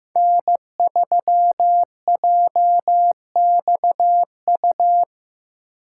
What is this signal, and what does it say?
Morse code, heard as a keyed tone, about 15 wpm, 700 Hz. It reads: N3JXU
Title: USA Call Signs - 5 Characters - 2x Character Spacing - 15wpm